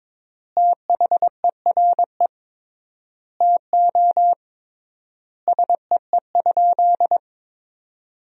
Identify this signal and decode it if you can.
Morse code, heard as a keyed tone, about 22 wpm, 700 Hz. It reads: THERE TO SEE?
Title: 3x Word-Spacing - Sentences from Top 100 Words - 22wpm